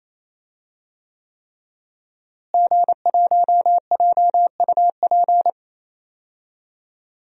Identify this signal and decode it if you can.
Morse code, heard as a keyed tone, about 27 wpm, 700 Hz. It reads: G1JUP
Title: DXCC-100 Call Signs - 5 Characters - Repeated 3x - 28wpm